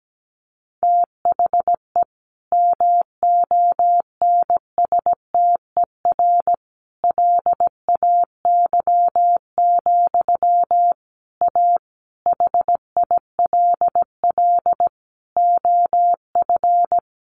Morse code, heard as a keyed tone, about 17 wpm, 700 Hz. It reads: THE MONSTER LAY, A HILL OF
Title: A Sound Of Thunder 17wpm